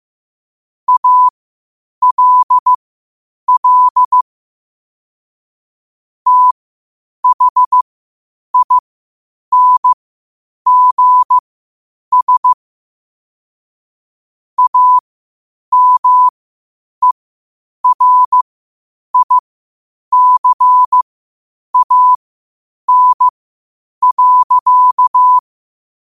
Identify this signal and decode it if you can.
Morse code, heard as a keyed tone, about 15 wpm, spaced as for 8 wpm, 1000 Hz. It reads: ALL THINGS AMERICAN.
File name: qotdfiles / 8wpm